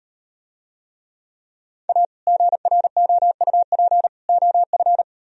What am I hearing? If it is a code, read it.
Morse code, heard as a keyed tone, about 38 wpm, 700 Hz. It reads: A GROUP OF